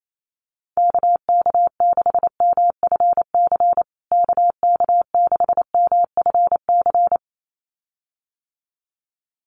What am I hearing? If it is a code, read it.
Morse code, heard as a keyed tone, about 28 wpm, 700 Hz. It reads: KK6MFC KK6MFC
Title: USA Call Signs Repeated 2x 28wpm